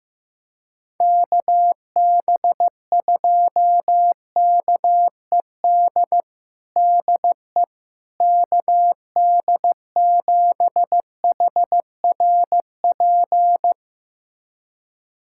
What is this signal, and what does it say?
Morse code, heard as a keyed tone, about 15 wpm, 700 Hz. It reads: KB2KED DE KD7HRP
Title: USA Callsign DE Callsign 15wpm